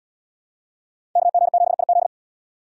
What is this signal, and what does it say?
Morse code, heard as a keyed tone, about 77 wpm, 700 Hz. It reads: KG7IZ